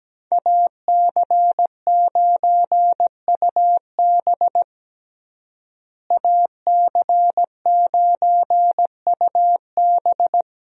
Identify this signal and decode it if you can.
Morse code, heard as a keyed tone, about 17 wpm, 700 Hz. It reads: AC9UB AC9UB
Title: USA Call Signs - 5 Characters - Repeated 2x - 17wpm